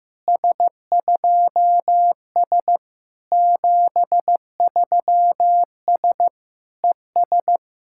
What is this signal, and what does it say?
Morse code, heard as a keyed tone, about 16 wpm, 700 Hz. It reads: S2S 73S ES
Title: SOTA QSO Practice 15wpm